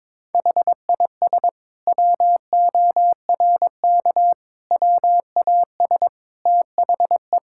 Morse code, heard as a keyed tone, about 21 wpm, 700 Hz. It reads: HIS WORK WAS THE
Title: Sentences from Top 1000 Words - Review Part 5 - 22wpm